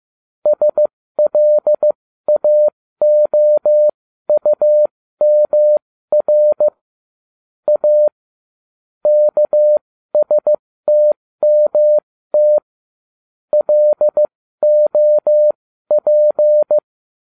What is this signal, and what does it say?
Morse code, heard as a keyed tone, about 15 wpm, spaced as for 13 wpm, 600 Hz. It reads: SLAOUMR A KSTMT LOP
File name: cw / ubacw09 1